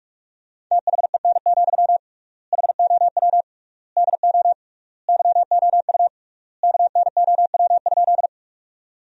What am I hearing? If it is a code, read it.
Morse code, heard as a keyed tone, about 42 wpm, 700 Hz. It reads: THEN, HOW DO YOU KNOW?
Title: 3x Word-Spacing - Sentences from Top 100 Words - 45wpm